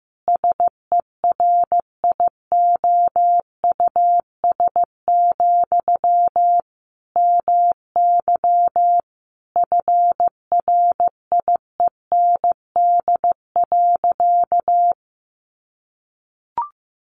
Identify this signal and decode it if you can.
Morse code, heard as a keyed tone, about 15 wpm, 700 Hz. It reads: SERIOUS, MY FRIEND.
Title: Sentences from Top 1100 Words 15wpm